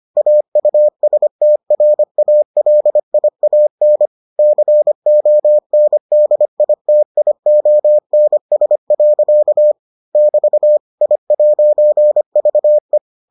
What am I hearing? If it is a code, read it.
Morse code, heard as a keyed tone, about 25 wpm, 600 Hz. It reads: AUSTRALIAN CONDITIONS. <BT> I'VE